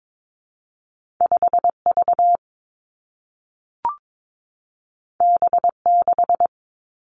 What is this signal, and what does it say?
Morse code, heard as a keyed tone, about 22 wpm, 700 Hz. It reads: 5 V B 6
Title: Two Challenging Letter-Number Combinations 22wpm